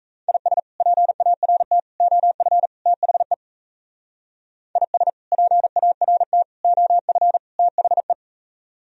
Morse code, heard as a keyed tone, about 37 wpm, 700 Hz. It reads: IS PART OF THE IS PART OF THE